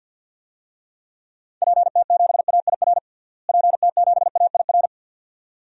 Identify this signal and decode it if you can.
Morse code, heard as a keyed tone, about 51 wpm, 700 Hz. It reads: PT7AIR PT7AIR
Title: DXCC-100 Call Signs - 6 Characters - Repeated 2x - 50wpm